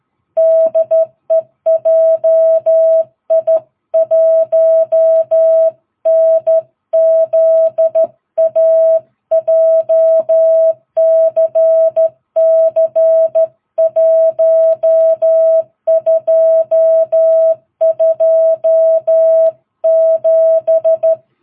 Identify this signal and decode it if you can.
Morse code, heard as a keyed tone, about 13 wpm, 660 Hz. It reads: DE JI1NZA JCC 1227
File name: jh8cbh / 2022cw